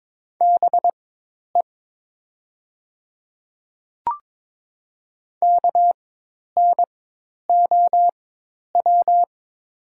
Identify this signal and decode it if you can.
Morse code, heard as a keyed tone, about 22 wpm, spaced as for 10 wpm, 700 Hz. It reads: BE KNOW
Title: 4x Character Spacing - Top 100 Words - 22wpm